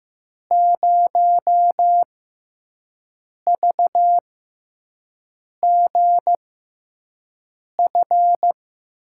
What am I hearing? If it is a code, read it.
Morse code, heard as a keyed tone, about 15 wpm, spaced as for 5 wpm, 700 Hz. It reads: 0VGF
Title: USA Call Signs - 5 Characters - 6x Character Spacing - 15wpm